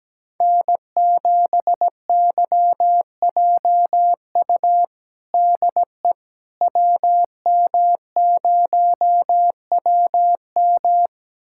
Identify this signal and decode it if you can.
Morse code, heard as a keyed tone, about 17 wpm, 700 Hz. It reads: N7YJU DE WM0WM
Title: USA Callsign DE Callsign - 5 Characters - 17wpm